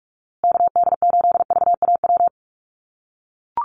Morse code, heard as a keyed tone, about 44 wpm, 690 Hz. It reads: KD8VAW
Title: USA Call Signs Repeated 3x 45wpm